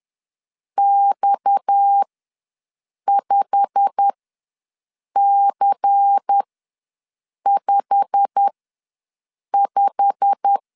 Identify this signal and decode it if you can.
Morse code, heard as a keyed tone, about 10 wpm, spaced as for 6 wpm, 800 Hz. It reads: X5C55
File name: LEZIONI / 8 YJ45 / 8 YJ45 ws 10wpm 3843